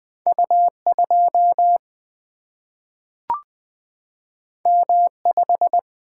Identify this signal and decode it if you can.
Morse code, heard as a keyed tone, about 20 wpm, 700 Hz. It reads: U2 M5
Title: Single-Letter Single-Number 20wpm